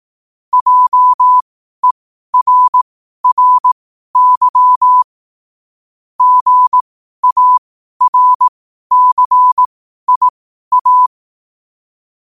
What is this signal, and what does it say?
Morse code, heard as a keyed tone, about 18 wpm, spaced as for 12 wpm, 1000 Hz. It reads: JERRY GARCIA
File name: qotdfiles / 12wpm